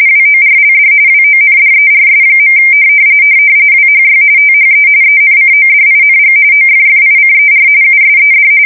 RTTY, about 45.45 baud. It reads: WORLD WONDERS. WHY THE ENCODING OFFICER USED
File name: fsk170 / USS-New-Jersey